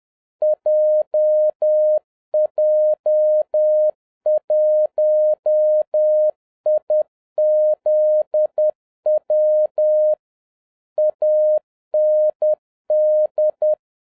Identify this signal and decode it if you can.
Morse code, heard as a keyed tone, about 10 wpm, 610 Hz. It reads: JJ1IZW AND
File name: 2011-07-mp3 / n46 JOHN K3TN 10wpm